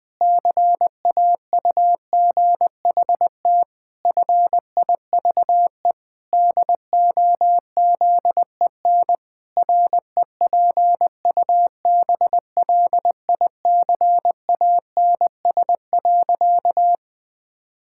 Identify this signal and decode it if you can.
Morse code, heard as a keyed tone, about 20 wpm, 700 Hz. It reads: CAUGHT FIVE DOZEN REPUBLICANS.
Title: Pangram Challenge v2 - 20wpm